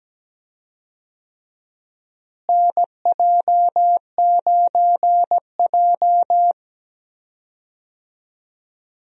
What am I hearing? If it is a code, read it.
Morse code, heard as a keyed tone, about 17 wpm, 700 Hz. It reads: NJ9J
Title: USA Call Signs - 4 Characters - Repeated 2x - 17wpm